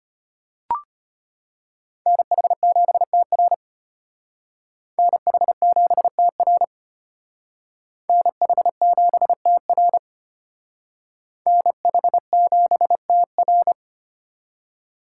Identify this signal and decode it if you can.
Morse code, heard as a keyed tone, about 32 wpm, 700 Hz. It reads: NH7TR NH7TR NH7TR NH7TR